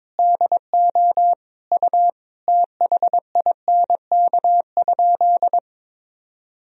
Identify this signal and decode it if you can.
Morse code, heard as a keyed tone, about 22 wpm, 700 Hz. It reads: DO U THINK?
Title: Sentences from Top 200 Words and Abbreviations 22wpm